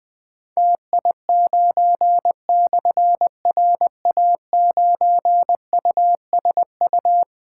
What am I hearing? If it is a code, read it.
Morse code, heard as a keyed tone, about 20 wpm, 700 Hz. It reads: TI9/RA9USU
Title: Difficult Call Signs 20wpm